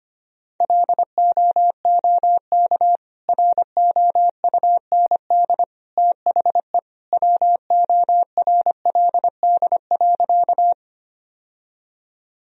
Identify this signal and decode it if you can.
Morse code, heard as a keyed tone, about 25 wpm, 700 Hz. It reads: LOOK ROUND THE WORLD.
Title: Sentences from Top 1000 Words - Review Part 1 - 25wpm